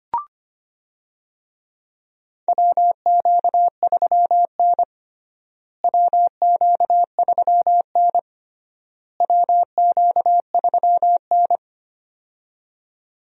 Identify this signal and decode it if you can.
Morse code, heard as a keyed tone, about 25 wpm, 700 Hz. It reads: WQ3N WQ3N WQ3N
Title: USA Call Signs - 4 Characters - Repeated 3x - 25wpm